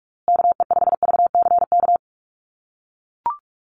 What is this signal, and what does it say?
Morse code, heard as a keyed tone, about 44 wpm, 690 Hz. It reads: KE5VCK
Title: USA Call Signs Repeated 2x 45wpm